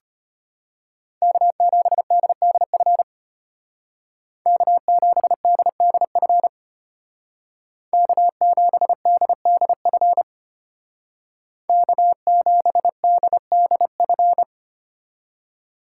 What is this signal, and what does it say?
Morse code, heard as a keyed tone, about 32 wpm, 700 Hz. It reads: K7DDF K7DDF K7DDF K7DDF